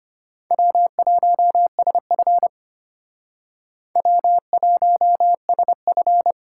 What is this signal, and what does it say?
Morse code, heard as a keyed tone, about 34 wpm, 700 Hz. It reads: W1SF W1SF